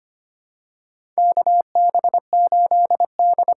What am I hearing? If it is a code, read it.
Morse code, heard as a keyed tone, about 25 wpm, 700 Hz. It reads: KB8D
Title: USA Call Signs - 6 Characters - 6x Character Spacing - 25wpm